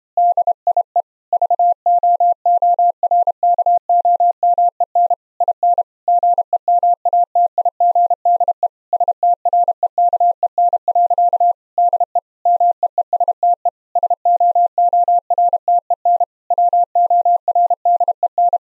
Morse code, heard as a keyed tone, about 31 wpm, 700 Hz. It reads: DIE VOORKOMEN IN GEMATIGDE STREKEN. DE MEESTE SOORTEN WORDEN
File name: Zendamateurnieuws en een stukje geseinde morse